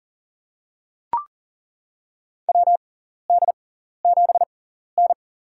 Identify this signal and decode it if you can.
Morse code, heard as a keyed tone, about 40 wpm, spaced as for 14 wpm, 700 Hz. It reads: WD7N
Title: USA Call Signs - 6x Character Spacing - 40wpm